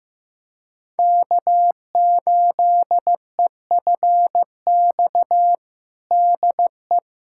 Morse code, heard as a keyed tone, about 15 wpm, 700 Hz. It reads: K8EFX DE
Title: USA Callsign DE Callsign - 5 Characters - 15wpm